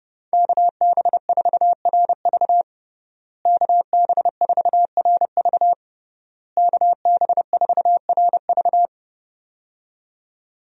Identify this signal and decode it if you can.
Morse code, heard as a keyed tone, about 30 wpm, 700 Hz. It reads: KB4RV KB4RV KB4RV
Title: USA Call Signs - 5 Characters - Repeated 3x - 30wpm